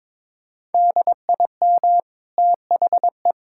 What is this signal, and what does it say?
Morse code, heard as a keyed tone, about 22 wpm, 690 Hz. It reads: DIM THE
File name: practice / 22wpm / Easy Sentences with Hints - 22wpm